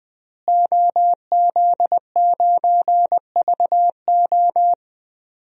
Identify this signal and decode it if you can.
Morse code, heard as a keyed tone, about 20 wpm, 700 Hz. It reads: OZ9VO
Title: DXCC-100 Call Signs - 5 Characters - Repeated 2x - 20wpm